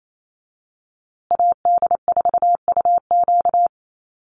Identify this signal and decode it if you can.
Morse code, heard as a keyed tone, about 28 wpm, 700 Hz. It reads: AD4UQ
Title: USA Call Signs Repeated 3x 28wpm